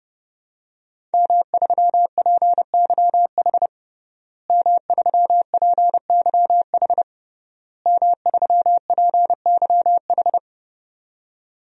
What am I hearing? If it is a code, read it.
Morse code, heard as a keyed tone, about 31 wpm, 700 Hz. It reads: M3PYH M3PYH M3PYH
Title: DXCC-100 Call Signs - 5 Characters - Repeated 3x - 30wpm